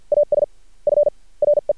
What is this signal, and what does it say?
Morse code, heard as a keyed tone, about 51 wpm, spaced as for 33 wpm, 590 Hz. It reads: AS FRE
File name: archived / CRSnet